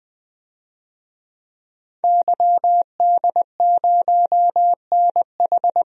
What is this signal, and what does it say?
Morse code, heard as a keyed tone, about 20 wpm, 700 Hz. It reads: YD0NH
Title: DXCC-100 Call Signs - 5 Characters - 20wpm